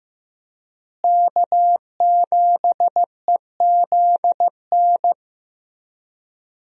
Morse code, heard as a keyed tone, about 15 wpm, 700 Hz. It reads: K7EZN
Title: USA Call Signs - 5 Characters - Repeated 2x - 15wpm